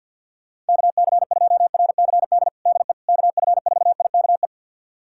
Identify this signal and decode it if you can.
Morse code, heard as a keyed tone, about 52 wpm, 700 Hz. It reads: KC2RCD DE KF4IKE
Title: USA Callsign DE Callsign 50wpm